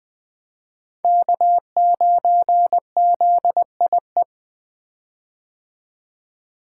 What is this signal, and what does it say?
Morse code, heard as a keyed tone, about 21 wpm, 700 Hz. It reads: K9ZIE
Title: 2023 POTA Activator Call Signs - 20wpm